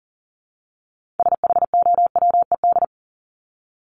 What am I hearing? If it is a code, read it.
Morse code, heard as a keyed tone, about 40 wpm, 700 Hz. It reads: SHOWED
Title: ED Suffix - Top 100 Words with Base Words 40wpm